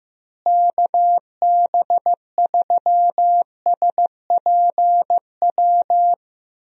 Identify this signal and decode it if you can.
Morse code, heard as a keyed tone, about 16 wpm, 700 Hz. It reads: KB3SPW
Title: USA Call Signs - 6 Characters - Repeated 3x - 15wpm